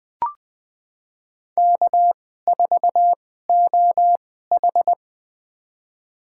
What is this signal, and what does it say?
Morse code, heard as a keyed tone, about 20 wpm, spaced as for 14 wpm, 700 Hz. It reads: K4OH
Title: USA Call Signs - 4 Characters - 2x Character Spacing - 20wpm